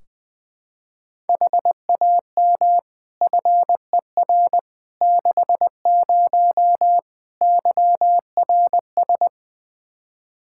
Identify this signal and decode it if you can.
Morse code, heard as a keyed tone, about 21 wpm, 700 Hz. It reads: HAM FER 60 YRS
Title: K1USN SST QSO Practice 20wpm